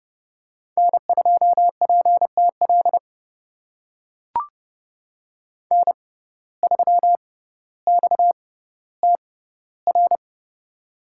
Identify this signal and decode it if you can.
Morse code, heard as a keyed tone, about 31 wpm, spaced as for 15 wpm, 700 Hz. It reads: N2PTL N3XTR
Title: USA Call Signs - 5 Characters - 6x Character Spacing - 30wpm